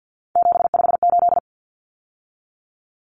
Morse code, heard as a keyed tone, about 50 wpm, 700 Hz. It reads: 758
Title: Common Cut Numbers - 3-Digit 50wpm